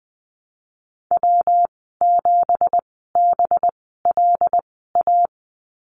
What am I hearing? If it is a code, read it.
Morse code, heard as a keyed tone, about 20 wpm, spaced as for 14 wpm, 700 Hz. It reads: W7BLA
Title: 186 Lesson 30g - Letter b - Call Signs 20wpm